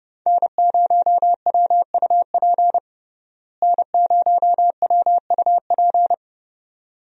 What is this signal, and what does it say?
Morse code, heard as a keyed tone, about 30 wpm, 700 Hz. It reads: N0WUP N0WUP
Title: USA Call Signs - 5 Characters - Repeated 2x - 30wpm